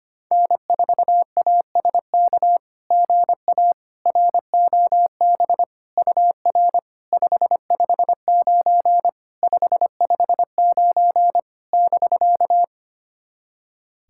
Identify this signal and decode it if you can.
Morse code, heard as a keyed tone, about 25 wpm, 700 Hz. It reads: N4ASK GA ROB UR 559 559 <BK>
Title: SOTA QSO Practice 25wpm